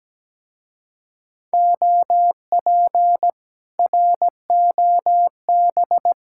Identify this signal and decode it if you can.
Morse code, heard as a keyed tone, about 17 wpm, 700 Hz. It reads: OP ROB
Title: Speed-Racing Common QSO Phrases - 17wpm